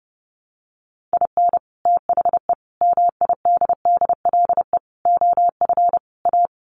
Morse code, heard as a keyed tone, about 30 wpm, 700 Hz. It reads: IN THE MIDDLE OF A
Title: Sets of 5 Words 30wpm